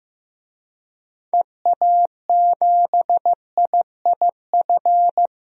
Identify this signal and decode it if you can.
Morse code, heard as a keyed tone, about 15 wpm, 700 Hz. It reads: EA7IIF
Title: DXCC-100 Call Signs - 6 Characters - 15wpm